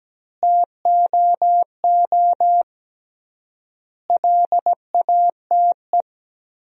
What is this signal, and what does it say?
Morse code, heard as a keyed tone, about 17 wpm, 700 Hz. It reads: TOO LATE
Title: 3x Word-Spacing - Sentences from Top 300 Words - 17wpm